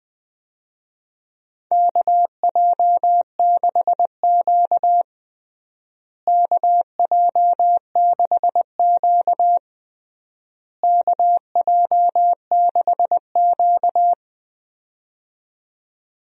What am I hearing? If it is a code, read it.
Morse code, heard as a keyed tone, about 21 wpm, 700 Hz. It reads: KJ6Q KJ6Q KJ6Q
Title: USA Call Signs - 4 Characters - Repeated 3x - 20wpm